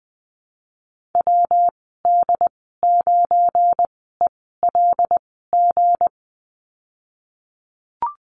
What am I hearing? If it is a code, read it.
Morse code, heard as a keyed tone, about 20 wpm, spaced as for 15 wpm, 700 Hz. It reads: WD9ELG
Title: 173 Lesson 28g - Number 9 - Call Signs 20wpm